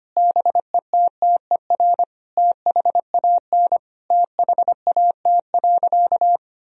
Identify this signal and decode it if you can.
Morse code, heard as a keyed tone, about 25 wpm, 700 Hz. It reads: BETTER THAN THAT.
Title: Sentences from Top 1000 Words - Review Part 4 - 25wpm